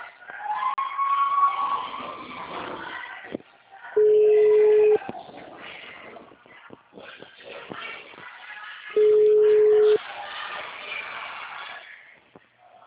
Ringback tone (Europe, 425 Hz).
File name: out- -2001- - - .23